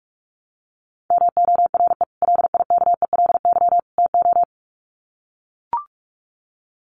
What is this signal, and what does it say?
Morse code, heard as a keyed tone, about 44 wpm, 700 Hz. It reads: MORE LIKELY TO
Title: Sets of 3 Words 45wpm